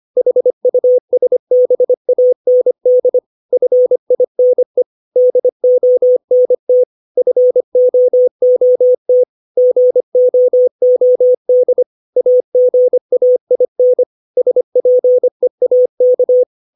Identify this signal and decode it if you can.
Morse code, heard as a keyed tone, about 25 wpm, 500 Hz. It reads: HUSBAND FINE DONT FOOT GOOD AGAIN SPEAK
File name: 632 words / 632 Words 25WPM